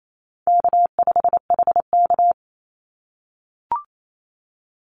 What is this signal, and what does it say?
Morse code, heard as a keyed tone, about 28 wpm, 700 Hz. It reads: K5HK
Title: Easy Call Signs - 28wpm